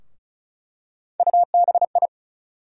Morse code, heard as a keyed tone, about 34 wpm, 700 Hz. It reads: UBI
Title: USA Call Sign Postfixes - 3 Letters 35wpm